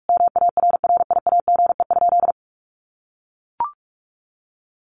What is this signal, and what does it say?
Morse code, heard as a keyed tone, about 44 wpm, 700 Hz. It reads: MARRIAGE?
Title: Sentences from Top 1200 Words 45wpm